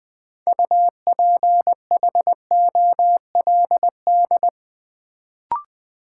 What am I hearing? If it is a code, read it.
Morse code, heard as a keyed tone, about 20 wpm, 700 Hz. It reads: UPHOLD
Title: Build A Word - 2 Letters At A Time - 6-Letter Words - 20wpm